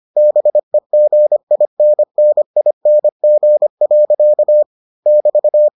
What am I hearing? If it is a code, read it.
Morse code, heard as a keyed tone, about 25 wpm, 600 Hz. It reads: BEGINNING. <BT>